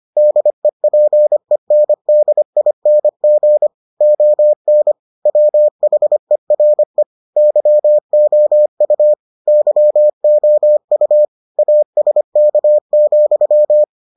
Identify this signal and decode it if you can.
Morse code, heard as a keyed tone, about 25 wpm, 600 Hz. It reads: DEPENDING ON WHERE YOU YOU ASK,